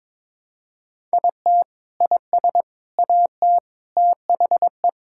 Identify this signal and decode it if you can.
Morse code, heard as a keyed tone, about 22 wpm, 700 Hz. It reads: IT IS AT THE
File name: practice / 22wpm / Sentences with 4-Word Phrases - 22wpm